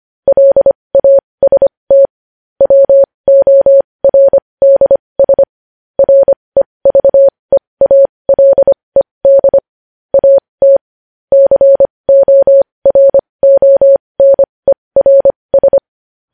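Morse code, heard as a keyed tone, about 25 wpm, spaced as for 20 wpm, 550 Hz. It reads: LAST WORDS REVEALED AT CORONERS
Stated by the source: Morse code news headlines sent at 20 wpm.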